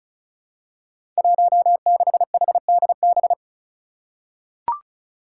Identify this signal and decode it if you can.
Morse code, heard as a keyed tone, about 36 wpm, 700 Hz. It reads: 16HDB
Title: Five Challenging Letter-Number Combinations 35wpm